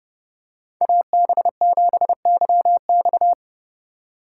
Morse code, heard as a keyed tone, about 29 wpm, 700 Hz. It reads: AB7YX